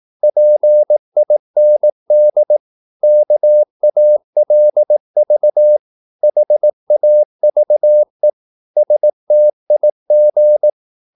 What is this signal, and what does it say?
Morse code, heard as a keyed tone, about 18 wpm, 600 Hz. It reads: PIND KALV HAVE STIG
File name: Korte ord DK 18wpm